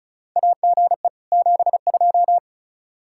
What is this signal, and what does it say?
Morse code, heard as a keyed tone, about 36 wpm, 700 Hz. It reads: AGE 72
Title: K1USN SST QSO Practice 35wpm